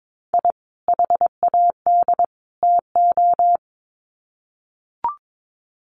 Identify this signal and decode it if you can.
Morse code, heard as a keyed tone, about 22 wpm, 700 Hz. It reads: I HAD TO
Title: Sets of 3 Words 22wpm